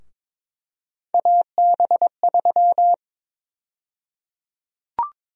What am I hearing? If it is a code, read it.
Morse code, heard as a keyed tone, about 22 wpm, 700 Hz. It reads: AB3
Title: USA Call Sign Prefixes - Repeated 2x - 22wpm